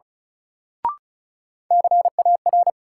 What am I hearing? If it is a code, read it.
Morse code, heard as a keyed tone, about 36 wpm, 700 Hz. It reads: CAR
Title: Sets of 5 Words Encore - New Word Drill - 35wpm